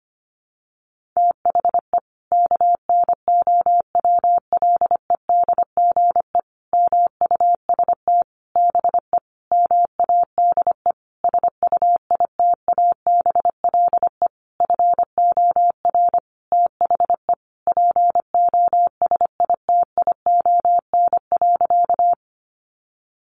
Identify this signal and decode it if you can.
Morse code, heard as a keyed tone, about 25 wpm, 700 Hz. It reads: THE KNOWLEDGE MUST BE MADE SUITABLE FOR THE POSITION.
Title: Sentences from Top 1400 Words 25wpm